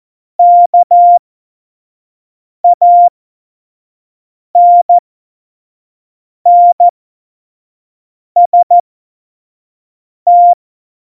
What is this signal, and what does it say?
Morse code, heard as a keyed tone, about 14 wpm, spaced as for 5 wpm, 700 Hz. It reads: KANNST
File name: Morsekurs / Witze / Witz 23. Juli 2019, 25 F70